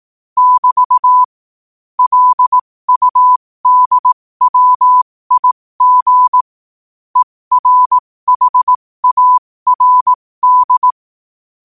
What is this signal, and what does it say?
Morse code, heard as a keyed tone, about 18 wpm, spaced as for 15 wpm, 1000 Hz. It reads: <BT> LUDWIG ERHARD